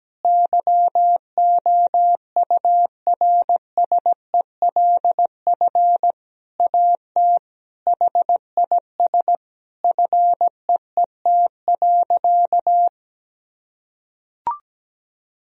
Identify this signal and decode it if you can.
Morse code, heard as a keyed tone, about 17 wpm, 700 Hz. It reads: YOURSELF AT HIS FEET.
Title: Sentences from Top 1000 Words - Review Part 1 - 17wpm